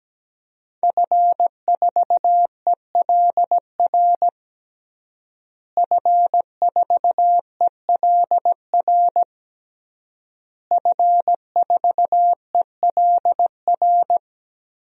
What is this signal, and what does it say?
Morse code, heard as a keyed tone, about 17 wpm, 700 Hz. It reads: F4ELR F4ELR F4ELR
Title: DXCC-100 Call Signs - 5 Characters - Repeated 3x - 17wpm